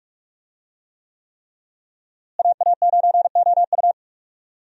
Morse code, heard as a keyed tone, about 41 wpm, 700 Hz. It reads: AA9OU
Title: USA Call Signs - 5 Characters - Repeated 3x - 45wpm